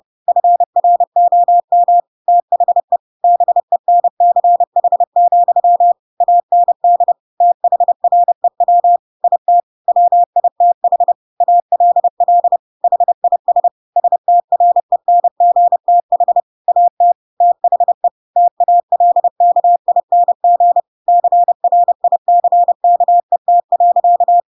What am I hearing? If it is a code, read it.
Morse code, heard as a keyed tone, about 30 wpm, 700 Hz. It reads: FROM THE BENCH, AND THREW IT WITH ALL HIS STRENGTH AT THE TALKING CRICKET.
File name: Pinocchio / Pinocchio ch4 30 30 ik1hge